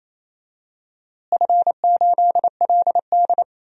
Morse code, heard as a keyed tone, about 28 wpm, 700 Hz. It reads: F8LD
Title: DXCC-100 Call Signs - 4 Characters - Repeated 3x - 28wpm